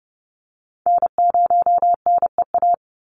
Morse code, heard as a keyed tone, about 30 wpm, 700 Hz. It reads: N0NEA
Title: USA Call Signs 30wpm